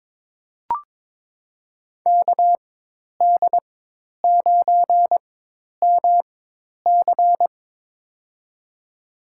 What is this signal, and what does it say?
Morse code, heard as a keyed tone, about 22 wpm, spaced as for 10 wpm, 700 Hz. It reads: KD9MC
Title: USA Call Signs - 5 Characters - 4x Character Spacing - 22wpm